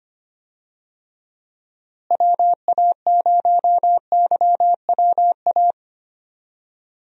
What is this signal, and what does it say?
Morse code, heard as a keyed tone, about 25 wpm, 700 Hz. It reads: WA0YWA
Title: USA Call Signs - 6 Characters - Repeated 2x - 25wpm